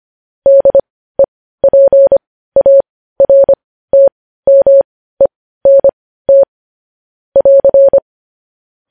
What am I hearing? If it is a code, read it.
Morse code, heard as a keyed tone, about 25 wpm, spaced as for 15 wpm, 550 Hz. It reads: DEPARTMENT <AR>
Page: news-headlines-in-morse-code-at-15-wpm